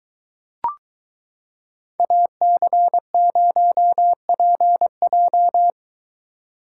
Morse code, heard as a keyed tone, about 22 wpm, 700 Hz. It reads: AC0PJ